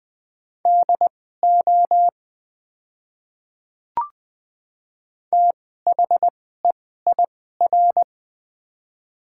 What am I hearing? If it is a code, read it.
Morse code, heard as a keyed tone, about 20 wpm, spaced as for 15 wpm, 700 Hz. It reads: DO THEIR
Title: 2x Character Spacing - Top 100 Words - 20wpm